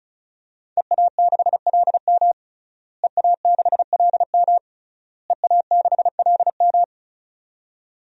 Morse code, heard as a keyed tone, about 36 wpm, 700 Hz. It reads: EA6LM EA6LM EA6LM
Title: DXCC-100 Call Signs - 5 Characters - Repeated 3x - 35wpm